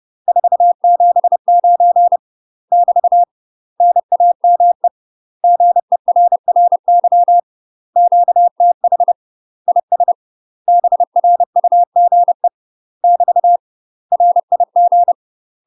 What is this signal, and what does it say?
Morse code, heard as a keyed tone, about 30 wpm, 700 Hz. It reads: 479 <BT> NAME GERRY QTH IS BRUGE <BT> RIG
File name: UK MASTER QSOs TRAINING / 30WPM / UK MASTER QSOs 0002